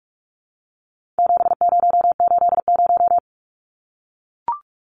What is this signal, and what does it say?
Morse code, heard as a keyed tone, about 44 wpm, 700 Hz. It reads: 7080
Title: Numbers - 4-Digit 45wpm